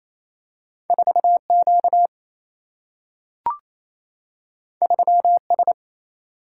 Morse code, heard as a keyed tone, about 28 wpm, 700 Hz. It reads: 4 Q 3 S
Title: Single-Letter Single-Number 28wpm